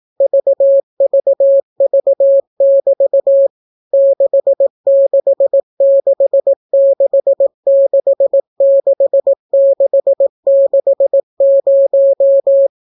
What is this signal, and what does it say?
Morse code, heard as a keyed tone, about 18 wpm, 550 Hz. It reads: VVV<BT> 666666660